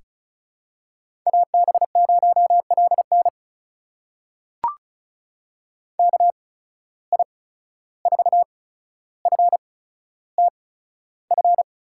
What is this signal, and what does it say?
Morse code, heard as a keyed tone, about 36 wpm, spaced as for 14 wpm, 700 Hz. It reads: AB0LN KI4FTF
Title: USA Call Signs - 8x Character Spacing - 35wpm